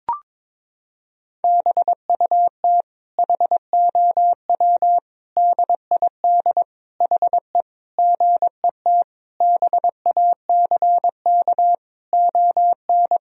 Morse code, heard as a keyed tone, about 22 wpm, 700 Hz. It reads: BUT HOW DID HE GET BACK ON
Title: Sentences from Top 1000 Words - Review Part 4 - 22wpm